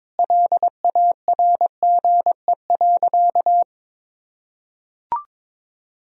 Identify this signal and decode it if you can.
Morse code, heard as a keyed tone, about 22 wpm, 700 Hz. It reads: LARGE.
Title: Sentences from Top 1000 Words - Review Part 1 - 22wpm